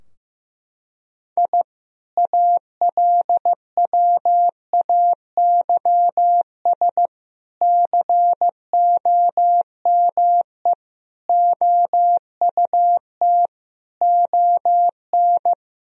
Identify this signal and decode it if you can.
Morse code, heard as a keyed tone, about 15 wpm, 700 Hz. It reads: I ALWAYS COME OUT ON
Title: Sentences from Top 1000 Words - Review Part 2 - 15wpm